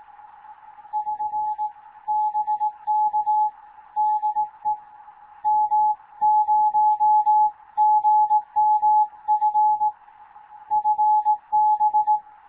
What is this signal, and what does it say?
Morse code, heard as a keyed tone, about 18 wpm, 850 Hz. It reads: <SN> BK DE M0GMF FB
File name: Morze-HA2MN / hangp / 2 11 GYAKORLAT